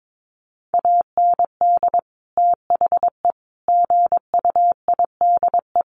Morse code, heard as a keyed tone, about 22 wpm, 700 Hz. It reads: AND THE GUIDE
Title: Sentences from Top 600 Words 22wpm